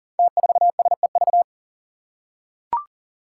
Morse code, heard as a keyed tone, about 39 wpm, 700 Hz. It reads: T4SEV
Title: Five Challenging Letter-Number Combinations 40wpm